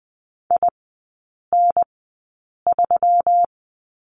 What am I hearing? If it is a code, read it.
Morse code, heard as a keyed tone, about 20 wpm, spaced as for 8 wpm, 700 Hz. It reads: IN3
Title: 121 Lesson 20e - Number 3 - Three Characters 20wpm